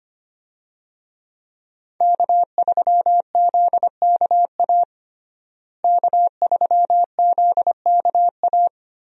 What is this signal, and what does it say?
Morse code, heard as a keyed tone, about 25 wpm, 700 Hz. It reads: K3ZKA K3ZKA
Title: USA Call Signs - 5 Characters - Repeated 2x - 25wpm